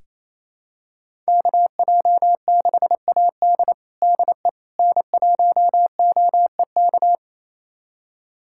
Morse code, heard as a keyed tone, about 27 wpm, 700 Hz. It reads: KJ6AD DE N1OEK
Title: USA Callsign DE Callsign - 5 Characters - 28wpm